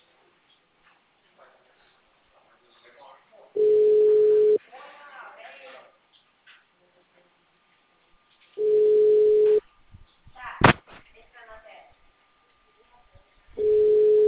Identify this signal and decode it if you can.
Ringback tone (Europe, 425 Hz), with speech.
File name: out- -2004- - - .72395